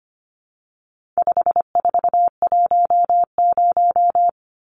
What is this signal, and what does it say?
Morse code, heard as a keyed tone, about 25 wpm, 700 Hz. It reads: 5410
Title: Numbers - 4-Digit 25wpm